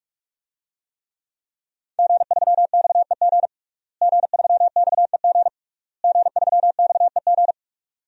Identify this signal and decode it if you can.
Morse code, heard as a keyed tone, about 46 wpm, 700 Hz. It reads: G3XEG G3XEG G3XEG
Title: DXCC-100 Call Signs - 5 Characters - Repeated 3x - 45wpm